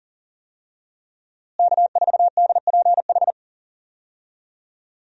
Morse code, heard as a keyed tone, about 40 wpm, 700 Hz. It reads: K4DPH
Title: USA Call Signs - 5 Characters - Repeated 3x - 40wpm